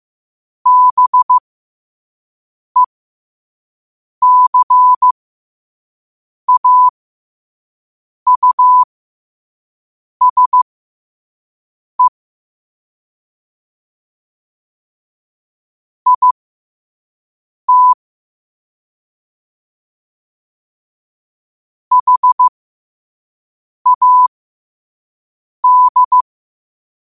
Morse code, heard as a keyed tone, about 15 wpm, spaced as for 5 wpm, 1000 Hz. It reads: BECAUSE IT HAD
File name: qotdfiles / 5wpm